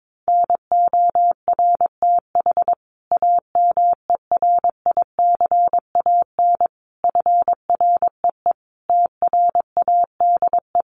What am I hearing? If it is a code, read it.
Morse code, heard as a keyed tone, about 22 wpm, 700 Hz. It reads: NORTH AMERICAN FREE TRADE
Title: Sets of 5 Words Encore 22wpm